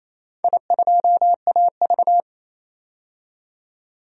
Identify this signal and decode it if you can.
Morse code, heard as a keyed tone, about 28 wpm, 690 Hz. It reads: I2AV
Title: DXCC-100 Call Signs - 4 Characters - 28wpm